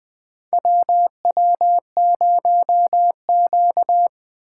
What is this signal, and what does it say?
Morse code, heard as a keyed tone, about 20 wpm, 700 Hz. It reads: WW0Q
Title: USA Call Signs - 8x Character Spacing - 20wpm